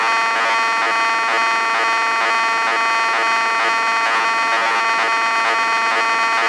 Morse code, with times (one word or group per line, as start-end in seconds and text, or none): none